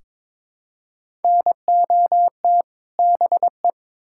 Morse code, heard as a keyed tone, about 22 wpm, 700 Hz. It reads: NOT BE
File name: Build A Phrase - 2-Word Phrases - 22wpm